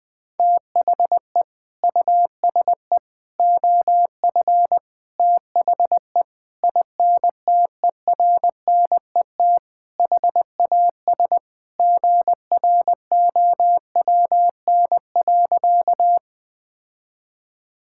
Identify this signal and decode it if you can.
Morse code, heard as a keyed tone, about 20 wpm, 700 Hz. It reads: THE USE OF THE INTERNET HAS GROWN.
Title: Sentences with 4-Word Phrases - 20wpm